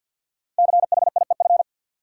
Morse code, heard as a keyed tone, about 52 wpm, 700 Hz. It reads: CHIEF
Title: Sets of 5 Words Encore - New Word Drill - 50wpm